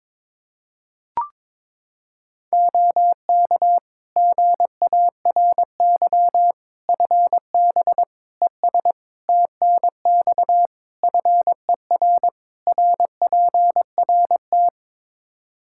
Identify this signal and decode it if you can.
Morse code, heard as a keyed tone, about 22 wpm, 700 Hz. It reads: OK GARY FB ES TNX FER RPRT
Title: Practice QSO - Exchange 2 - Info - 22wpm